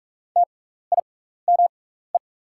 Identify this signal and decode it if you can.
Morse code, heard as a keyed tone, about 45 wpm, spaced as for 15 wpm, 690 Hz. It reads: TIME
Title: 6x Character Spacing - Top 100 Words - 45wpm